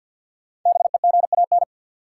Morse code, heard as a keyed tone, about 53 wpm, 700 Hz. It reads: BEGAN
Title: Sentences from Top 300 Words - New Word Drill - 50wpm